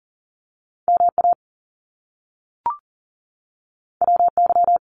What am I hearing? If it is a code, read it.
Morse code, heard as a keyed tone, about 38 wpm, 700 Hz. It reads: MA WY
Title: US States 40wpm